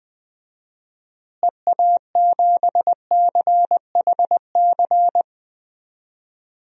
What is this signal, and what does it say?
Morse code, heard as a keyed tone, about 20 wpm, 700 Hz. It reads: EA7CHC
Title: DXCC-100 Call Signs - 6 Characters - Repeated 3x - 20wpm